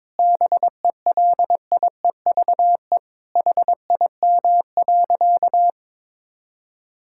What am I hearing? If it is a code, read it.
Morse code, heard as a keyed tone, about 21 wpm, 700 Hz. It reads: BELIEVE HIM.
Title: Sentences from Top 1000 Words - Review Part 5 - 22wpm